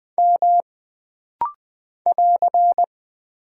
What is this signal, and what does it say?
Morse code, heard as a keyed tone, about 20 wpm, 700 Hz. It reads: M <AR>
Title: Single Character (Letter:Number:Punctuation:Prosign) - Rapid-Fire - 20wpm